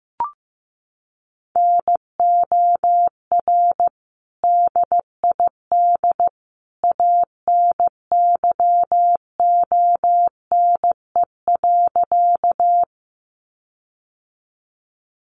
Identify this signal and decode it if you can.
Morse code, heard as a keyed tone, about 15 wpm, 700 Hz. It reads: NOR DID ANYONE.
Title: Sentences from Top 1500 Words 15wpm